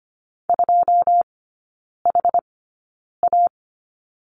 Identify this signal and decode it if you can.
Morse code, heard as a keyed tone, about 25 wpm, spaced as for 9 wpm, 690 Hz. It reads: 2HA
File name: Lesson 14e - Number 2 - Three Characters 25wpm